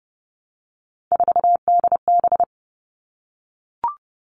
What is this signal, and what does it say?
Morse code, heard as a keyed tone, about 30 wpm, 700 Hz. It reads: 4DB
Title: All Cut Numbers - 3-Digit 30wpm